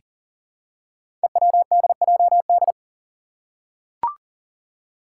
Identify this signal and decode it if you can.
Morse code, heard as a keyed tone, about 41 wpm, 700 Hz. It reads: EWDJD
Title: Five Challenging Letter-Number Combinations 40wpm